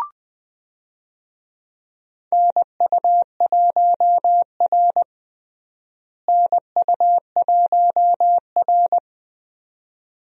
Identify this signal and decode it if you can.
Morse code, heard as a keyed tone, about 21 wpm, 700 Hz. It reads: NU1R NU1R
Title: USA Call Signs - 4 Characters - Repeated 2x - 20wpm